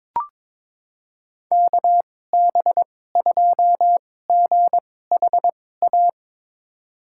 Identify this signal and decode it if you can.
Morse code, heard as a keyed tone, about 22 wpm, spaced as for 16 wpm, 700 Hz. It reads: KB2GHA
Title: USA Call Signs - 6 Characters - 2x Character Spacing - 22wpm